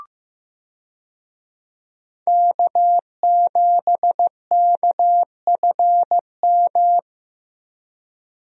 Morse code, heard as a keyed tone, about 16 wpm, 700 Hz. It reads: K7KFM
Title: USA Call Signs - 5 Characters - Repeated 2x - 15wpm